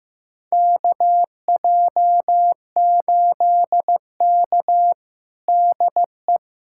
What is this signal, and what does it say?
Morse code, heard as a keyed tone, about 15 wpm, 700 Hz. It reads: KJ8K DE
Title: USA Callsign DE Callsign - 4 Characters - 15wpm